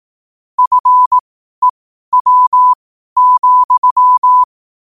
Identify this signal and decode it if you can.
Morse code, heard as a keyed tone, about 18 wpm, spaced as for 12 wpm, 1000 Hz. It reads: FEW,
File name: qotdfiles / 12wpm